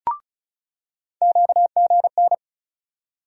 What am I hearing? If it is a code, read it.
Morse code, heard as a keyed tone, about 36 wpm, 700 Hz. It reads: QGN
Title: USA Call Sign Postfixes - 3 Letters 35wpm